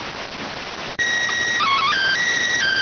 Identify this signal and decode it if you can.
SSTV (slow-scan television), PD 290.